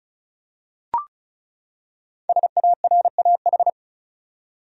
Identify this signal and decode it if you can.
Morse code, heard as a keyed tone, about 36 wpm, 700 Hz. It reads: SARAH
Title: USA Common Names - Max 5 Letters - 35wpm